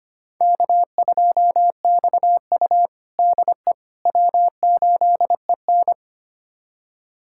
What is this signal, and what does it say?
Morse code, heard as a keyed tone, about 25 wpm, 700 Hz. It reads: K2XU DE W8EN
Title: USA Callsign DE Callsign - 4 Characters - 25wpm